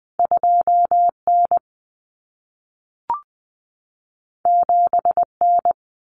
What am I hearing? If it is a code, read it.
Morse code, heard as a keyed tone, about 20 wpm, 700 Hz. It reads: 2N 7N
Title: Common Cut Numbers - 2-Digit 20wpm